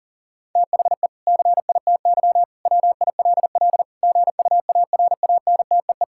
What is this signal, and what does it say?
Morse code, heard as a keyed tone, about 40 wpm, 700 Hz. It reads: THE CITY WILL GUARANTEE
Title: Sentences from Top 1600 Words 40wpm